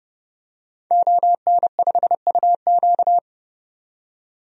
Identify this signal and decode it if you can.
Morse code, heard as a keyed tone, about 30 wpm, 700 Hz. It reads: ON5UQ
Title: DXCC-100 Call Signs - 5 Characters - 30wpm